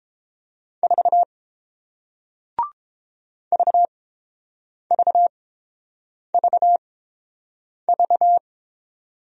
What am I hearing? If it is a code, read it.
Morse code, heard as a keyed tone, about 30 wpm, spaced as for 7 wpm, 690 Hz. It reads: 4 VVVV